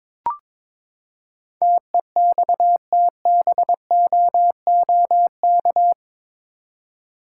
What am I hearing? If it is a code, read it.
Morse code, heard as a keyed tone, about 22 wpm, 700 Hz. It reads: TEXTBOOK
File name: practice / 22wpm / Eight Letter Words - 22wpm